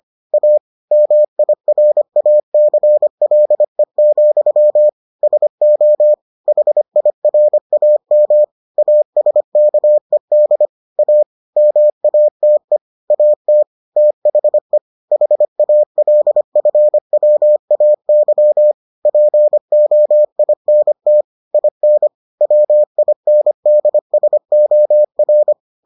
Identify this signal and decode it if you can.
Morse code, heard as a keyed tone, about 25 wpm, 600 Hz. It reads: A MIRACLE, SO HIRAM ASKED A MATE AT THE HALFWAY POINT IN WINDSOR